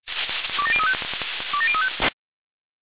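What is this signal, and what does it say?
Five-tone selective call (ZVEI), heard as quick runs of five tones, tones of about 70 ms. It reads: 27035 27035
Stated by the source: ZVEI five-tone selective call 27002 received on SDR.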